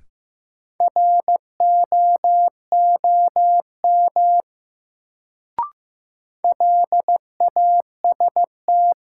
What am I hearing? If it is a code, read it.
Morse code, heard as a keyed tone, about 15 wpm, 700 Hz. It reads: ROOM LAST
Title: Sentences from Top 300 Words - New Word Drill - 15wpm